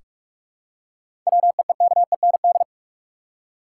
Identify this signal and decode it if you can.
Morse code, heard as a keyed tone, about 46 wpm, 700 Hz. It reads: WEEKEND
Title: Seven Letter Words - 45wpm